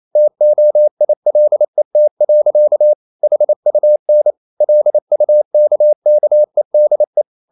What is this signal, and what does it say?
Morse code, heard as a keyed tone, about 28 wpm, 600 Hz. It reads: TOILET. HUN LUKKEDE
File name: Ferie QRQ 28wpm